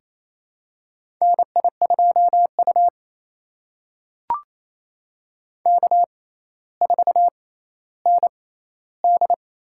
Morse code, heard as a keyed tone, about 28 wpm, spaced as for 14 wpm, 700 Hz. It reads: NI2U K4ND
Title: USA Call Signs - 4 Characters - 6x Character Spacing - 28wpm